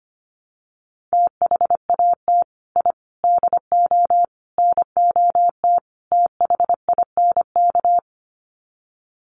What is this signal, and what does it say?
Morse code, heard as a keyed tone, about 25 wpm, 700 Hz. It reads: THAT I DO NOT THINK
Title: Sets of 5 Words Encore 25wpm